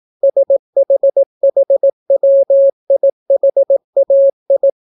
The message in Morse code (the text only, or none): SHHWIHAI